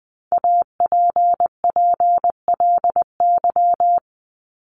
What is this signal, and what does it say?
Morse code, heard as a keyed tone, about 20 wpm, 700 Hz. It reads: APPLY
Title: ED Suffix - Top 100 Words with Base Words 20wpm